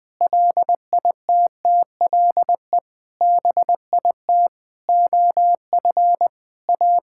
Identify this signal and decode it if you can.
Morse code, heard as a keyed tone, about 20 wpm, 700 Hz. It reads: LITTLE BIT OF A
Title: Sentences with 5-Word Phrases - Spoken Beginning - 20wpm